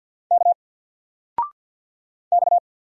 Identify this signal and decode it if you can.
Morse code, heard as a keyed tone, about 49 wpm, 700 Hz. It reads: K X
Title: Single Letter-Number 50wpm